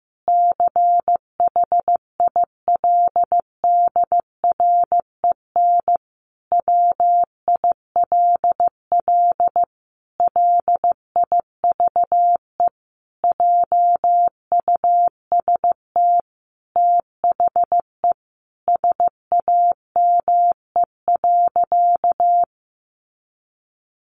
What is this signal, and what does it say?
Morse code, heard as a keyed tone, about 15 wpm, 700 Hz. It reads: CHILDREN WILL LIVE JUST THE SAME.
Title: Sentences from Top 300 Words 15wpm